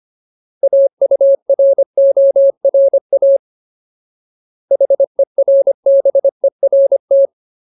Morse code, heard as a keyed tone, about 25 wpm, 550 Hz. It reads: AURORA HERBERT
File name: Ultimate Word List / 25WPM Ultimate